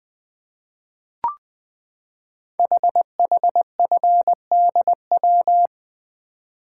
Morse code, heard as a keyed tone, about 20 wpm, 700 Hz. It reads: HHFDW
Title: Code Groups - 5 Letters - 20wpm